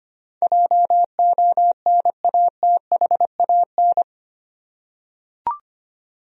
Morse code, heard as a keyed tone, about 25 wpm, 700 Hz. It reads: JONATHAN
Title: USA Common Names - 25wpm